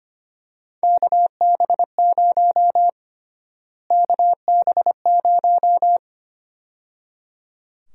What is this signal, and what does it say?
Morse code, heard as a keyed tone, about 25 wpm, 700 Hz. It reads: KB0 KB0
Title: USA Call Sign Prefixes - Repeated 2x - 25wpm